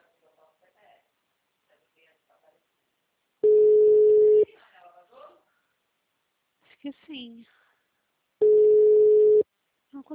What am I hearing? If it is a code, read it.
Ringback tone (Europe, 425 Hz), with speech.
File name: out- -2013- - - .1108